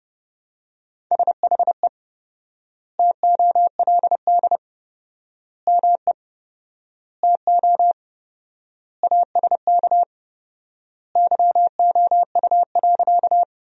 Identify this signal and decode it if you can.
Morse code, heard as a keyed tone, about 32 wpm, 700 Hz. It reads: SHE TOLD ME TO ASK YOU.
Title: 4x Word-Spacing - Sentences from Top 400 Words - 30wpm